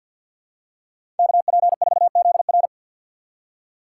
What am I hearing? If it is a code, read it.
Morse code, heard as a keyed tone, about 53 wpm, 700 Hz. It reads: KP4ZR
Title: USA Call Signs - 5 Characters - 8x Character Spacing - 50wpm